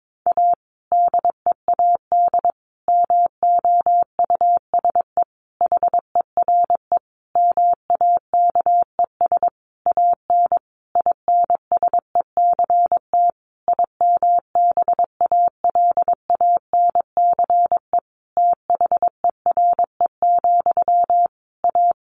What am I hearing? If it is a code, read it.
Morse code, heard as a keyed tone, about 22 wpm, 700 Hz. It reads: A DEAD MOUSE HERE MAKES AN INSECT IMBALANCE THERE, A
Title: A Sound Of Thunder 22wpm